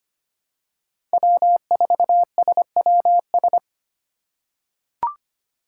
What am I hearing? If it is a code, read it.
Morse code, heard as a keyed tone, about 25 wpm, 700 Hz. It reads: W4SWS
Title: USA Call Signs - 5 Characters - 3x Character Spacing - 25wpm